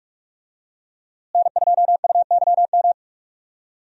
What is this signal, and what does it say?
Morse code, heard as a keyed tone, about 47 wpm, 700 Hz. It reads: N2UYM
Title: USA Call Signs - 5 Characters - Repeated 2x - 45wpm